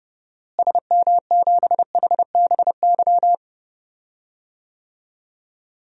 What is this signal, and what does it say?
Morse code, heard as a keyed tone, about 32 wpm, 700 Hz. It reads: SM7HBY
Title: DXCC-100 Call Signs - 6 Characters - Repeated 3x - 30wpm